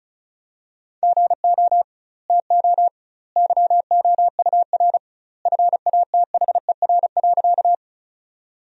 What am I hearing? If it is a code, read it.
Morse code, heard as a keyed tone, about 36 wpm, 700 Hz. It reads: GO TO YOUR FATHER.
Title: 2x Word-Spacing - Sentences from Top 300 Words - 35wpm